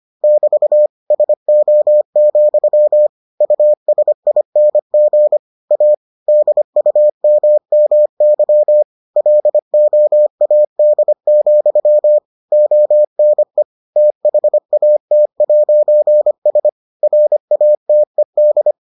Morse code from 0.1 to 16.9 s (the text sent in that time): <BT> SO, USING A DUMMY LOAD, ONE THAT'S